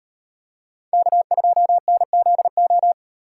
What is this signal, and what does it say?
Morse code, heard as a keyed tone, about 36 wpm, 700 Hz. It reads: K2NZO